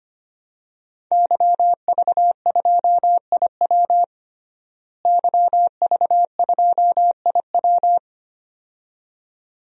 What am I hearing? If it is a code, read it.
Morse code, heard as a keyed tone, about 25 wpm, 700 Hz. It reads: YV2IW YV2IW
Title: DXCC-100 Call Signs - 5 Characters - Repeated 2x - 25wpm